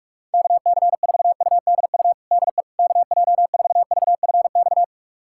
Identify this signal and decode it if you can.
Morse code, heard as a keyed tone, about 46 wpm, 700 Hz. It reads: KC4UDU DE KJ4VFX
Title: USA Callsign DE Callsign - 6 Characters - 45wpm